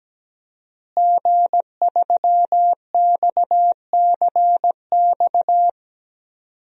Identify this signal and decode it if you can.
Morse code, heard as a keyed tone, about 17 wpm, 700 Hz. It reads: G3XCX
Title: DXCC-100 Call Signs - 5 Characters - Repeated 3x - 17wpm